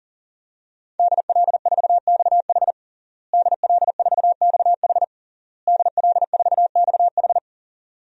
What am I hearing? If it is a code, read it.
Morse code, heard as a keyed tone, about 40 wpm, 700 Hz. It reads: DL4XH DL4XH DL4XH
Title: DXCC-100 Call Signs - 5 Characters - Repeated 3x - 40wpm